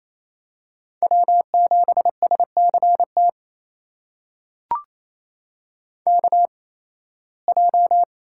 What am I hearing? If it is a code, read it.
Morse code, heard as a keyed tone, about 28 wpm, spaced as for 18 wpm, 700 Hz. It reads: W7SCT KJ
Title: USA Call Signs - 8x Character Spacing - 28wpm